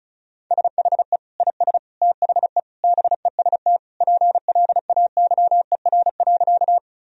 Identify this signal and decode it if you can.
Morse code, heard as a keyed tone, about 36 wpm, 700 Hz. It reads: SHE IS THE BEST PLAYER.
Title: Sentences from Top 1000 Words - Review Part 4 - 35wpm